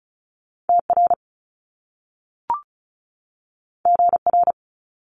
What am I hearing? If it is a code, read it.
Morse code, heard as a keyed tone, about 36 wpm, 700 Hz. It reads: TR GR
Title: Top Two Letter Combinations in Words 35wpm